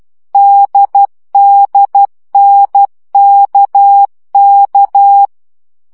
Morse code, heard as a keyed tone, about 12 wpm, 800 Hz. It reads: DDNKK
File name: CW gyakorlas n d k 10 baud